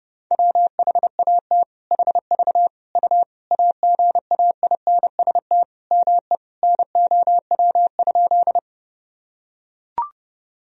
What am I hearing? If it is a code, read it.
Morse code, heard as a keyed tone, about 32 wpm, 700 Hz. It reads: WHAT HV U AGAINST ME NOW?
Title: Sentences from Top 400 Words and Abbreviations 30wpm